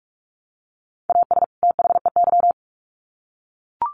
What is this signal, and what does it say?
Morse code, heard as a keyed tone, about 44 wpm, 700 Hz. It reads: AS THEY
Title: Sets of 2 Words 45wpm